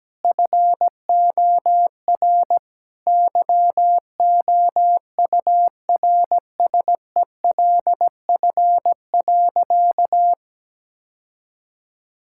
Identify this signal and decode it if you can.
Morse code, heard as a keyed tone, about 17 wpm, 700 Hz. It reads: FOR YOURSELF.
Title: Sentences from Top 1000 Words - Review Part 5 - 17wpm